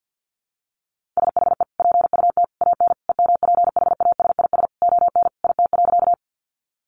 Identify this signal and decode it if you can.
Morse code, heard as a keyed tone, about 50 wpm, 700 Hz. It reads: SHE PUT AN EMPHASIS ON IT.
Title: Sentences from Top 1400 Words 50wpm